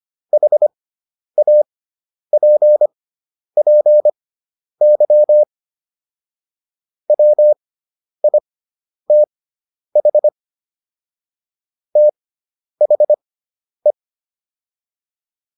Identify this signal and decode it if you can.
Morse code, heard as a keyed tone, about 25 wpm, spaced as for 10 wpm, 600 Hz. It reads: HAPPY WITH THE